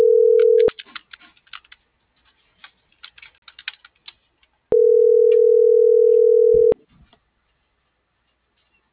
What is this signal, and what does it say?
Ringback tone (North America), 440 + 480 Hz.